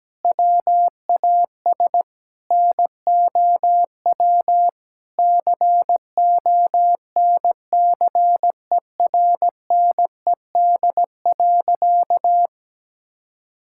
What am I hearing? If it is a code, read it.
Morse code, heard as a keyed tone, about 17 wpm, 700 Hz. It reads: WAS NOW CONCERNED.
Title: Sentences from Top 1000 Words - Review Part 6 - 17wpm